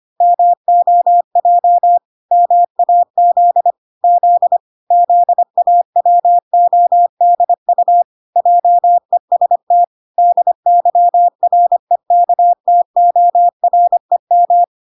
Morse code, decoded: MOJ MAZ Z ZAWODU JEST DYREKTOREM

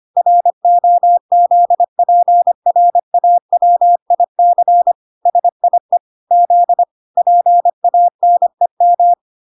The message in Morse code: ROZPRAWIC SIE Z PANEM

